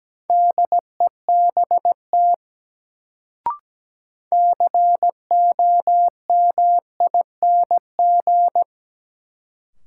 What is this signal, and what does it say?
Morse code, heard as a keyed tone, about 17 wpm, 700 Hz. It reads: DEBT COMING
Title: Sentences from Top 1300 Words - New Word Drill Alt - 17wpm